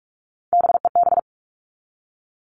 Morse code, heard as a keyed tone, about 44 wpm, 700 Hz. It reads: BEB
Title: All Cut Numbers - 3-Digit 45wpm